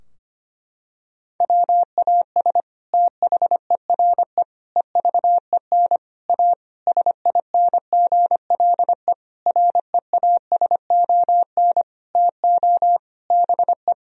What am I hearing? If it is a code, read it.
Morse code, heard as a keyed tone, about 25 wpm, 700 Hz. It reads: WAS THERE EVEN A SINGLE REASON TO BE
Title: Sentences from Top 1000 Words - Review Part 4 - 25wpm